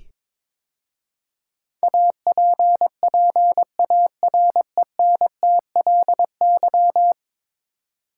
Morse code, heard as a keyed tone, about 22 wpm, 700 Hz. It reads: APPARENTLY
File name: practice / 22wpm / 10 Letter Words - First 3 Letters Given - 22wpm